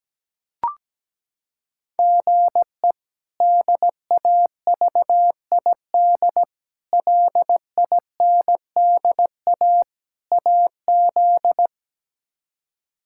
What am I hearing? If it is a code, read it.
Morse code, heard as a keyed tone, about 17 wpm, 700 Hz. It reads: GE DAVID LINDA AZ
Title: K1USN SST QSO Practice - 17wpm